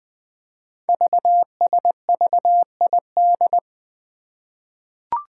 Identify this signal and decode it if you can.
Morse code, heard as a keyed tone, about 20 wpm, 700 Hz. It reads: VSVID
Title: Code Groups - 5 Letters - 20wpm